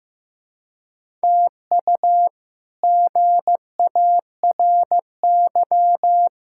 Morse code, heard as a keyed tone, about 15 wpm, 700 Hz. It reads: TU GARY
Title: K1USN SST QSO Practice - 15wpm